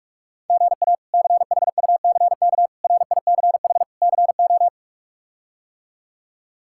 Morse code, heard as a keyed tone, about 46 wpm, 700 Hz. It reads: GA CHUCK RICH CO
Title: K1USN SST QSO Practice - 45wpm